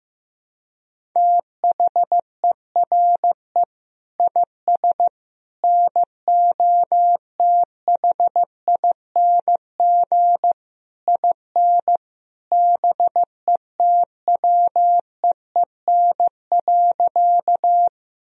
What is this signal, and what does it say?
Morse code, heard as a keyed tone, about 16 wpm, 700 Hz. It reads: THERE IS NOTHING IN BETWEEN.
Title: Sentences from Top 1000 Words - Review Part 1 - 15wpm